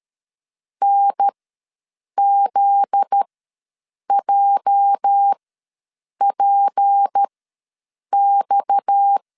Morse code, heard as a keyed tone, about 13 wpm, spaced as for 7 wpm, 800 Hz. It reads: NZJPX
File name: LEZIONI / 8 YJ45 / 8 YJ45 ws 12wpm 3146